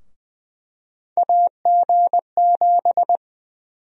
Morse code, heard as a keyed tone, about 20 wpm, 700 Hz. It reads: AG7
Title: USA Call Sign Prefixes - 20wpm